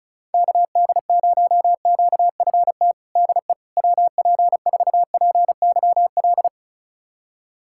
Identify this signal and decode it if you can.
Morse code, heard as a keyed tone, about 34 wpm, 700 Hz. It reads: KD0QFT DE WP4PYL
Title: USA Callsign DE Callsign - 6 Characters - 35wpm